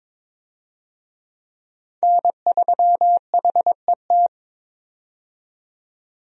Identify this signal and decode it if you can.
Morse code, heard as a keyed tone, about 22 wpm, 700 Hz. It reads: N3HET
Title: USA Call Signs - 5 Characters - 22wpm